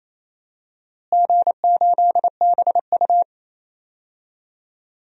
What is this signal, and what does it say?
Morse code, heard as a keyed tone, about 28 wpm, 700 Hz. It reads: G8BU
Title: DXCC-100 Call Signs - 4 Characters - Repeated 3x - 28wpm